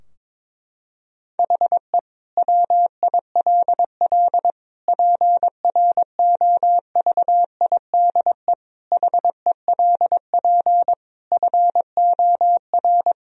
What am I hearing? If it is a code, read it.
Morse code, heard as a keyed tone, about 22 wpm, 700 Hz. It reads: HE WILL PROVIDE HELP FOR
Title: Sentences from Top 1000 Words - Review Part 3 - 22wpm